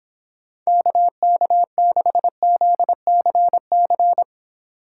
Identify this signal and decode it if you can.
Morse code, heard as a keyed tone, about 26 wpm, 700 Hz. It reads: KK6ZCC